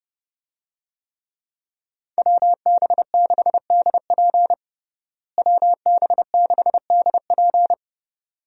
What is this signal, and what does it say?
Morse code, heard as a keyed tone, about 30 wpm, 700 Hz. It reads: WB6DP WB6DP
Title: USA Call Signs - 5 Characters - Repeated 2x - 30wpm